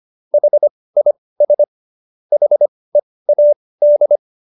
Morse code, heard as a keyed tone, about 25 wpm, spaced as for 18 wpm, 590 Hz. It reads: HIS HEAD